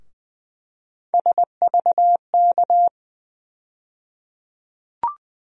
Morse code, heard as a keyed tone, about 20 wpm, 700 Hz. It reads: SVK
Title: USA Call Sign Postfixes - 3 Letters 20wpm